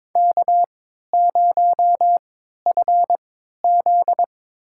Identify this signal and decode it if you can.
Morse code, heard as a keyed tone, about 22 wpm, spaced as for 12 wpm, 700 Hz. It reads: K0FZ
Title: USA Call Signs - 4 Characters - 3x Character Spacing - 22wpm